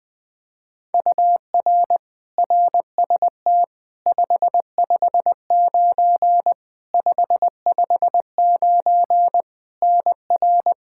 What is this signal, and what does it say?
Morse code, heard as a keyed tone, about 20 wpm, 700 Hz. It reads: UR RST 559 559 NR
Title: K1USN SST QSO Practice 20wpm